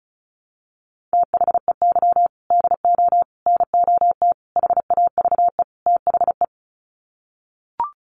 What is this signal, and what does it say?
Morse code, heard as a keyed tone, about 34 wpm, 700 Hz. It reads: THEY DO NOT HAVE THE
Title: Sets of 5 Words 35wpm